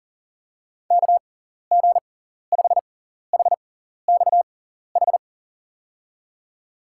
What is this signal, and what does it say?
Morse code, heard as a keyed tone, about 37 wpm, spaced as for 14 wpm, 700 Hz. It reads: KG5HXH
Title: USA Call Signs - 6 Characters - 6x Character Spacing - 40wpm